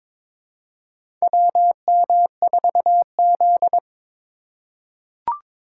Morse code, heard as a keyed tone, about 22 wpm, 700 Hz. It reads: WM4Z
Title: USA Call Signs - 4 Characters - 6x Character Spacing - 22wpm